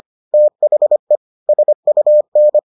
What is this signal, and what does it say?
Morse code, heard as a keyed tone, about 25 wpm, 600 Hz. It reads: THE SUN